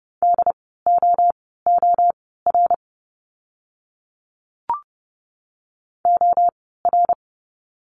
Morse code, heard as a keyed tone, about 30 wpm, spaced as for 17 wpm, 700 Hz. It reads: DOOR OR
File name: Lesson 17f - Letter u - Words 30wpm